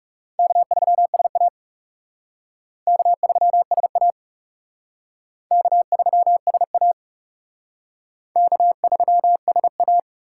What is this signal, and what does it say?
Morse code, heard as a keyed tone, about 37 wpm, 700 Hz. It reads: K3SA K3SA K3SA K3SA
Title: Speed Racing - USA 1x2 Call Signs 30wpm